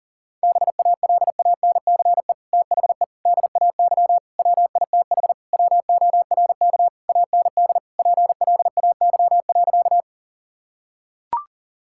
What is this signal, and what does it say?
Morse code, heard as a keyed tone, about 40 wpm, 700 Hz. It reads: BALANCE THE DAY WITH WORK AND PLAY.
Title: Sentences from Top 1000 Words - Review Part 6 - 40wpm